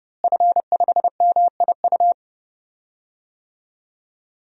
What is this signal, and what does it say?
Morse code, heard as a keyed tone, about 30 wpm, 700 Hz. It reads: F5MIU
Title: DXCC-100 Call Signs - 5 Characters - Repeated 2x - 30wpm